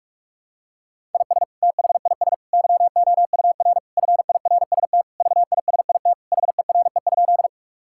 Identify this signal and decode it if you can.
Morse code, heard as a keyed tone, about 46 wpm, 700 Hz. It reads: IS THIS YOUR FIRST VISIT HERE?
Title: Sentences from Top 1000 Words - Review Part 6 - 45wpm